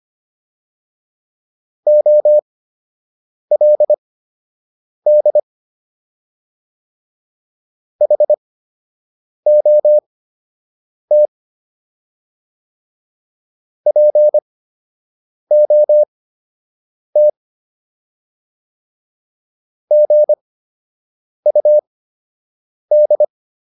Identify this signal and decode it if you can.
Morse code, heard as a keyed tone, about 25 wpm, spaced as for 7 wpm, 600 Hz. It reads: OLD HOT POT GUD